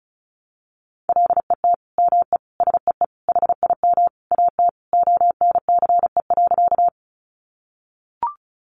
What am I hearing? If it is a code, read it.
Morse code, heard as a keyed tone, about 34 wpm, 700 Hz. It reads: LET ME SEE HIM AT ONCE.
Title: Sentences from Top 300 Words 35wpm